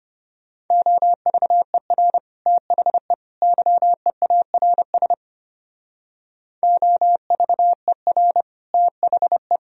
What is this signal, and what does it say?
Morse code, heard as a keyed tone, about 31 wpm, 700 Hz. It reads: OVER THE YEARS OVER THE
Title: Speed-Racing - Sets of 3 Words 25wpm